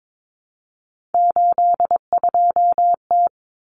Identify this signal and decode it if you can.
Morse code, heard as a keyed tone, about 22 wpm, 700 Hz. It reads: 82T
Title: Common Cut Numbers - 3-Digit 22wpm